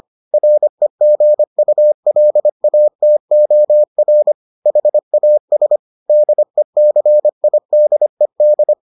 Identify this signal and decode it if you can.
Morse code, heard as a keyed tone, about 25 wpm, 600 Hz. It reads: REGULATOR HAS DECIDED